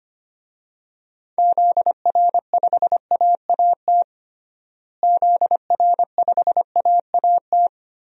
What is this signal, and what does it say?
Morse code, heard as a keyed tone, about 25 wpm, 700 Hz. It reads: ZR5AAT ZR5AAT
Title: DXCC-100 Call Signs - 6 Characters - Repeated 2x - 25wpm